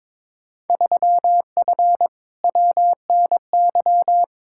Morse code, heard as a keyed tone, about 22 wpm, 700 Hz. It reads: 3F WNY
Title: Field Day Exchanges - 22wpm